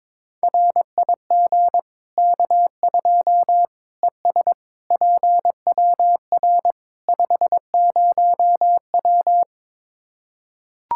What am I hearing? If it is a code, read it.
Morse code, heard as a keyed tone, about 22 wpm, 700 Hz. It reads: RIG K2 ES PWR 50W
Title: Practice QSO - Exchange 2 - Info - 22wpm